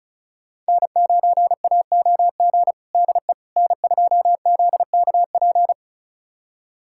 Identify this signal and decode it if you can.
Morse code, heard as a keyed tone, about 36 wpm, 700 Hz. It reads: N9AOG DE N2ZKP
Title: USA Callsign DE Callsign - 5 Characters - 35wpm